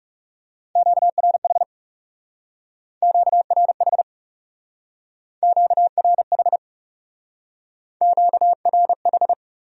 Morse code, heard as a keyed tone, about 38 wpm, 700 Hz. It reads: QRH QRH QRH QRH